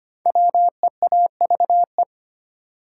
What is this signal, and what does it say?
Morse code, heard as a keyed tone, about 25 wpm, 700 Hz. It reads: WEAVE
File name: 5 Letter Words - First 3 Letters Given - 25wpm